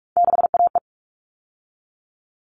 Morse code, heard as a keyed tone, about 44 wpm, 700 Hz. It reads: 6AE
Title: All Cut Numbers - 3-Digit 45wpm